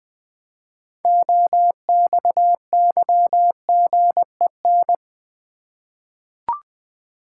Morse code, heard as a keyed tone, about 20 wpm, 700 Hz. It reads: OXYGEN
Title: 6 Letter Words - First 3 Letters Given - 20wpm